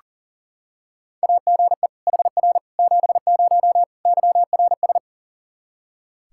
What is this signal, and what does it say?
Morse code, heard as a keyed tone, about 37 wpm, 700 Hz. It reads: AGE HR 70 YRS
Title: Practice QSO - Exchange 3 - More Info - 40wpm